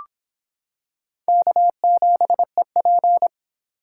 Morse code, heard as a keyed tone, about 26 wpm, 700 Hz. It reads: K7EP